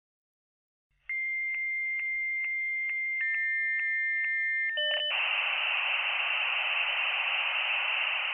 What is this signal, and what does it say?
Dial-up modem handshake.